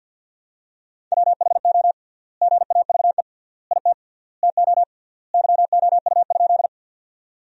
Morse code, heard as a keyed tone, about 53 wpm, 700 Hz. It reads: WHO GAVE IT TO YOU?
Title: 3x Word-Spacing - Sentences from Top 500 Words - 50wpm